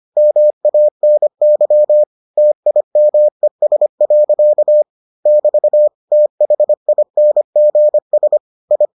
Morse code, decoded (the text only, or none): MANY TIMES. <BT> THINGS I